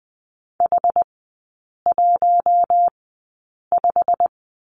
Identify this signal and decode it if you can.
Morse code, heard as a keyed tone, about 20 wpm, spaced as for 8 wpm, 700 Hz. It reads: H15
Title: 133 Lesson 22e - Question Mark - Three Characters 20wpm